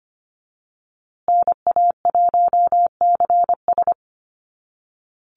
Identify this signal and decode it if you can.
Morse code, heard as a keyed tone, about 25 wpm, 700 Hz. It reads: NA1CS
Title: USA Call Signs Repeated 2x 25wpm